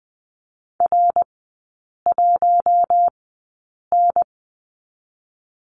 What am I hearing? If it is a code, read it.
Morse code, heard as a keyed tone, about 20 wpm, spaced as for 8 wpm, 700 Hz. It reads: R1N
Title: 133 Lesson 22e - Question Mark - Three Characters 20wpm